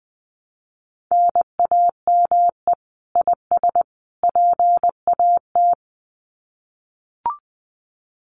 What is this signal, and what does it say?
Morse code, heard as a keyed tone, about 20 wpm, 700 Hz. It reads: NAME IS PAT
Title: Common QSO Phrases 20wpm